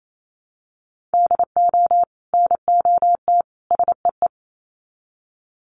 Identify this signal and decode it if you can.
Morse code, heard as a keyed tone, about 28 wpm, 700 Hz. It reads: DO NOT SEE
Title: Sets of 3 Words 28wpm